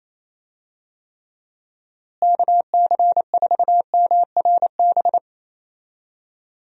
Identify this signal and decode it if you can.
Morse code, heard as a keyed tone, about 28 wpm, 700 Hz. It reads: KC4MRB
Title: USA Call Signs - 6 Characters - Repeated 2x - 28wpm